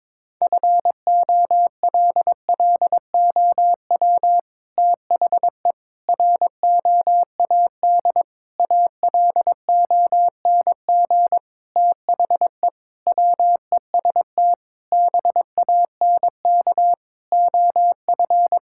Morse code, heard as a keyed tone, about 22 wpm, 700 Hz. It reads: FOLLOW THE ROAD ALONG THE WEST BANK OF
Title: Sentences from Top 1000 Words - Review Part 6 - 22wpm